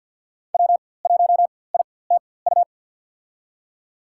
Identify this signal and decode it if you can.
Morse code, heard as a keyed tone, about 53 wpm, spaced as for 24 wpm, 700 Hz. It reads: W1ITU
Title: USA Call Signs - 4x Character Spacing - 50wpm